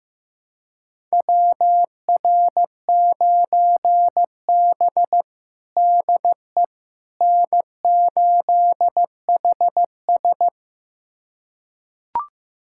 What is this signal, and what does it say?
Morse code, heard as a keyed tone, about 15 wpm, 700 Hz. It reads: WR9B DE N8HS
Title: USA Callsign DE Callsign - 4 Characters - 15wpm